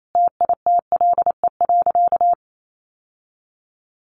Morse code, heard as a keyed tone, about 28 wpm, 700 Hz. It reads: TITLE.
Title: Sentences from Top 1200 Words 28wpm